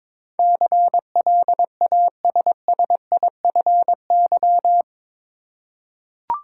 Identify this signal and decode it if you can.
Morse code, heard as a keyed tone, about 22 wpm, 700 Hz. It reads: CLASSIFY
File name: practice / 22wpm / Eight Letter Words - 22wpm